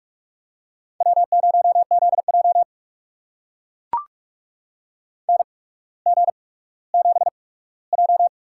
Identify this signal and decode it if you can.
Morse code, heard as a keyed tone, about 44 wpm, spaced as for 19 wpm, 700 Hz. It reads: W0ZJ NG7J
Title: USA Call Signs - 4 Characters - 8x Character Spacing - 45wpm